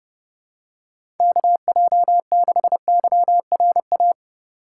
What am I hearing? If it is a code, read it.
Morse code, heard as a keyed tone, about 30 wpm, 700 Hz. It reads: KJ6YRA
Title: USA Call Signs - 2x Character Spacing - 30wpm